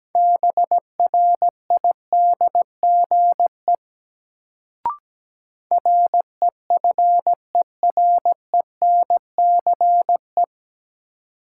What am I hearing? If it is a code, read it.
Morse code, heard as a keyed tone, about 17 wpm, 700 Hz. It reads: BRIDGE REFERENCE
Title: Sentences from Top 1200 Words - New Word Drill - 17wpm